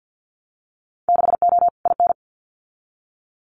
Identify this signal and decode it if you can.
Morse code, heard as a keyed tone, about 50 wpm, 700 Hz. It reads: 6O IN
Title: Winter Field Day Exchanges Encore 50wpm